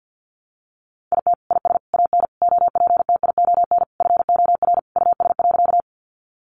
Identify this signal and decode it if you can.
Morse code, heard as a keyed tone, about 50 wpm, 700 Hz. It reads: IT IS AN OPTION FOR US.
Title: Sentences from Top 1300 Words 50wpm